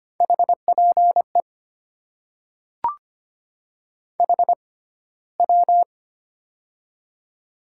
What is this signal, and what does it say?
Morse code, heard as a keyed tone, about 25 wpm, spaced as for 15 wpm, 690 Hz. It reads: HPE HW
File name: Top 90 QSO Elements - 6x Character Spacing - 25wpm